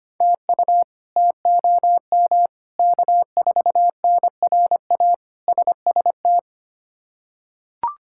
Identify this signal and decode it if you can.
Morse code, heard as a keyed tone, about 25 wpm, 700 Hz. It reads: TU TOM K4NRA SST
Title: K1USN SST QSO Practice - 25wpm